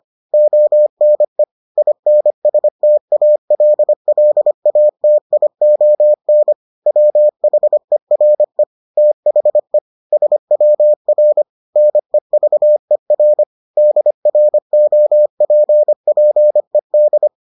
Morse code, heard as a keyed tone, about 25 wpm, 600 Hz. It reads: ONE INSTALLATION WHERE THE SWR NEVER DROPPED